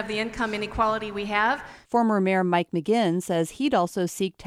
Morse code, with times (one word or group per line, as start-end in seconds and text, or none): none